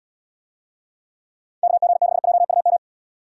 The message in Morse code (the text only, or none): KG6JUM